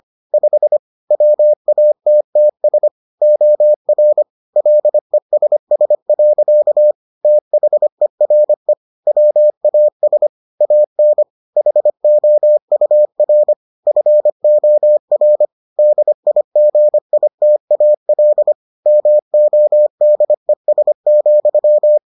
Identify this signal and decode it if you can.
Morse code, heard as a keyed tone, about 25 wpm, 600 Hz. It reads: 5 WATTS OR LESS. THERE WAS AN HOUR FOR DIGITAL MODES,